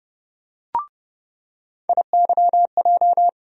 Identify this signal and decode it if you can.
Morse code, heard as a keyed tone, about 29 wpm, 700 Hz. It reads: IYJ
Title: USA Call Sign Postfixes - 3 Letters 30wpm